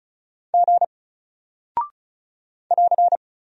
Morse code, heard as a keyed tone, about 36 wpm, 700 Hz. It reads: G <AR>
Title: Single Character (Letter:Number:Punctuation:Prosign) Alt 35wpm